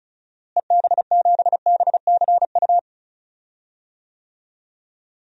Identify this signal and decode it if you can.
Morse code, heard as a keyed tone, about 36 wpm, 700 Hz. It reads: EB7BCU
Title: DXCC-100 Call Signs - 6 Characters - Repeated 3x - 35wpm